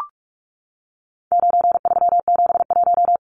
Morse code, heard as a keyed tone, about 44 wpm, 700 Hz. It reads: 9371
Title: Numbers - 4-Digit 45wpm